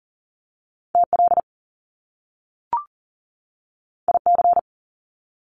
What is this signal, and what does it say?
Morse code, heard as a keyed tone, about 42 wpm, 700 Hz. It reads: TL IC
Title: Top Two Letter Combinations in Words 40wpm